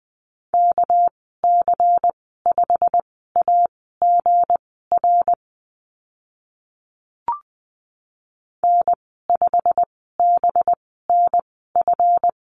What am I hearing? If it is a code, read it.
Morse code, heard as a keyed tone, about 20 wpm, spaced as for 15 wpm, 700 Hz. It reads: KC5AGR N5BNF
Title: 213 Lesson 34g - Number 8 - Call Signs 20wpm